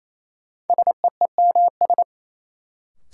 Morse code, heard as a keyed tone, about 28 wpm, 700 Hz. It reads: SEEMS
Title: Sets of 3 Words - New Word Drill - 28wpm